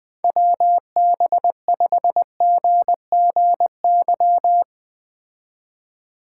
Morse code, heard as a keyed tone, about 20 wpm, 700 Hz. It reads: WB5GGY
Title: Speed Racing - USA 2x3 Call Signs 17wpm